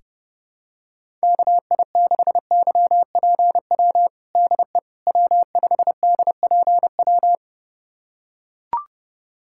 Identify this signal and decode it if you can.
Morse code, heard as a keyed tone, about 32 wpm, 700 Hz. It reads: KI6YPW DE W5DPW
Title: USA Callsign DE Callsign 30wpm